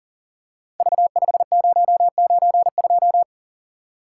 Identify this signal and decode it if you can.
Morse code, heard as a keyed tone, about 38 wpm, 690 Hz. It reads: V5092
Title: Code Groups - 5 Letter:Numbers - 40wpm